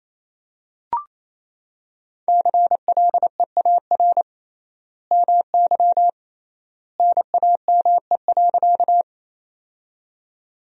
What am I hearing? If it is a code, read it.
Morse code, heard as a keyed tone, about 28 wpm, 700 Hz. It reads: CLEAR MY NAME.
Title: 3x Word-Spacing - Sentences from Top 500 Words - 28wpm